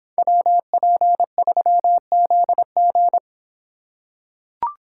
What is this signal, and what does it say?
Morse code, heard as a keyed tone, about 25 wpm, 700 Hz. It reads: WP3ZG